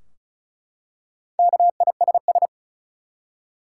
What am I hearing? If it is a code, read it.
Morse code, heard as a keyed tone, about 36 wpm, 700 Hz. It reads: KISS
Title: Four Letter Words - 35wpm